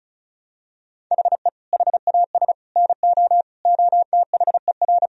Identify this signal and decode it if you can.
Morse code, heard as a keyed tone, about 34 wpm, 700 Hz. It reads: HE HAS NO OTHER
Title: Sentences from Top 1000 Words - Review Part 4 - 35wpm